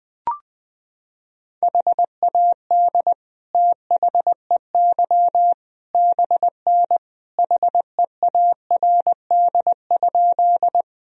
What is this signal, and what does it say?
Morse code, heard as a keyed tone, about 21 wpm, 700 Hz. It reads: HAD THEY BN HEARD?
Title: Sentences from Top 400 Words and Abbreviations 20wpm